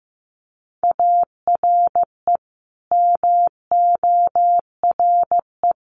Morse code, heard as a keyed tone, about 15 wpm, 700 Hz. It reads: ARE MORE
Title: Sets of 4 Words 15wpm